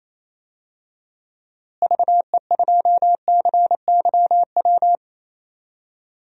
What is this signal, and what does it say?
Morse code, heard as a keyed tone, about 27 wpm, 700 Hz. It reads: VE2CYW
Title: DXCC-100 Call Signs - 6 Characters - Repeated 3x - 28wpm